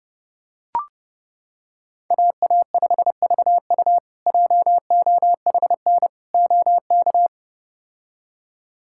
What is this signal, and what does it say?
Morse code, heard as a keyed tone, about 30 wpm, 700 Hz. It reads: AA5VU JOHN OK
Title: K1USN SST QSO Practice - 30wpm